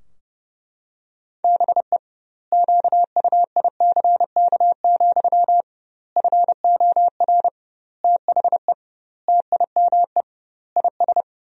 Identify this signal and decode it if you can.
Morse code, heard as a keyed tone, about 30 wpm, 700 Hz. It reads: BE QUICK, FOR THE TIME IS
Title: 2x Word-Spacing - Sentences from Top 500 Words - 30wpm